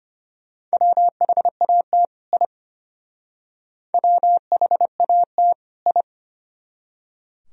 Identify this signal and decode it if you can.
Morse code, heard as a keyed tone, about 32 wpm, 700 Hz. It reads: WHAT I WHAT I